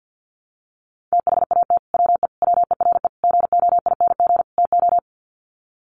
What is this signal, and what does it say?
Morse code, heard as a keyed tone, about 50 wpm, 700 Hz. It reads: THAT WE WERE GOING TO
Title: Sets of 5 Words Encore2 50wpm